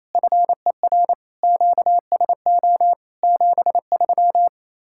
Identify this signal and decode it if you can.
Morse code, heard as a keyed tone, about 37 wpm, 700 Hz. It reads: FER QSO 73